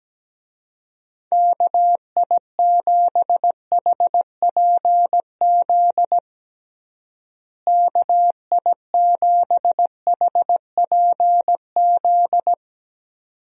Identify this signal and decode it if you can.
Morse code, heard as a keyed tone, about 17 wpm, 700 Hz. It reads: KI7HPZ KI7HPZ
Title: USA Call Signs - 6 Characters - Repeated 2x - 17wpm